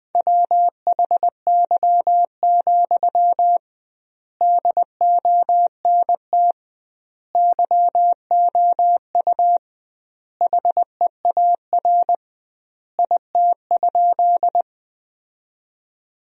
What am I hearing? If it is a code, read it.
Morse code, heard as a keyed tone, about 21 wpm, 700 Hz. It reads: WHY, DONT YOU HEAR IT?
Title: 2x Word-Spacing - Sentences from Top 300 Words - 20wpm